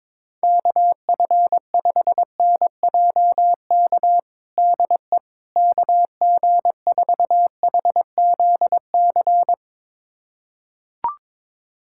Morse code, heard as a keyed tone, about 23 wpm, 700 Hz. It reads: KF5NJK DE KG4HZC
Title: USA Callsign DE Callsign - 6 Characters - 22wpm